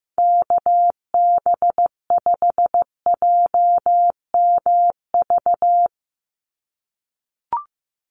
Morse code, heard as a keyed tone, about 15 wpm, 700 Hz. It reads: KB5JMV
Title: USA Call Signs 15wpm